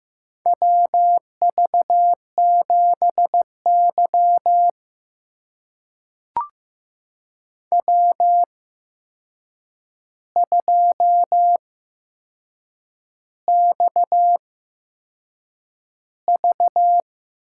Morse code, heard as a keyed tone, about 15 wpm, spaced as for 6 wpm, 700 Hz. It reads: WV7Y W2XV
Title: USA Call Signs - 4 Characters - 8x Character Spacing - 15wpm